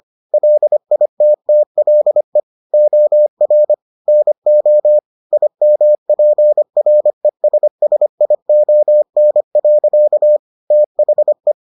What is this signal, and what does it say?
Morse code, heard as a keyed tone, about 25 wpm, 600 Hz. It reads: LITTLE OR NO IMPRESSION. THE